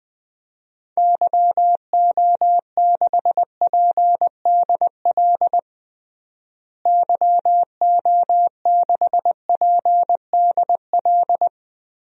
Morse code, heard as a keyed tone, about 20 wpm, 700 Hz. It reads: YO6PDL YO6PDL
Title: DXCC-100 Call Signs - 6 Characters - Repeated 2x - 20wpm